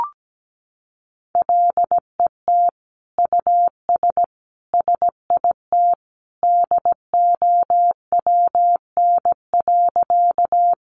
Morse code, heard as a keyed tone, about 17 wpm, 700 Hz. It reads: LET US SIT DOWN.
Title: Sentences from Top 500 Words 17wpm